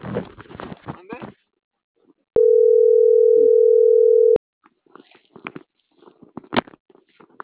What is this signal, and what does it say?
Ringback tone (North America), 440 + 480 Hz.